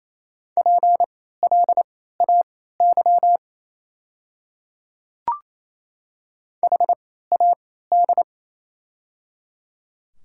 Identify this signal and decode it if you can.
Morse code, heard as a keyed tone, about 28 wpm, spaced as for 16 wpm, 700 Hz. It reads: PLAY HAD
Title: 3x Character Spacing - Top 200 Words - 28wpm